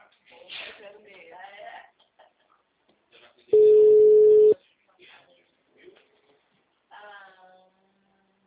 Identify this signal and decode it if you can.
Ringback tone (Europe, 425 Hz), with speech.